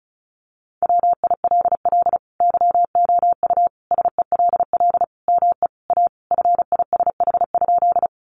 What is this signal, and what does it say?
Morse code, heard as a keyed tone, about 34 wpm, 700 Hz. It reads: WILL YOU SELL ME A FISH?
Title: Sentences from Top 1100 Words 35wpm